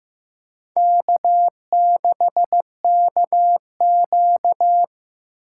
Morse code, heard as a keyed tone, about 15 wpm, 700 Hz. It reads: K6KQ
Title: USA Call Signs - 4 Characters - 8x Character Spacing - 15wpm